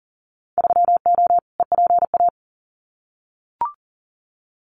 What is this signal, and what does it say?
Morse code, heard as a keyed tone, about 40 wpm, 700 Hz. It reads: 3O EPA
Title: Winter Field Day Exchanges 40wpm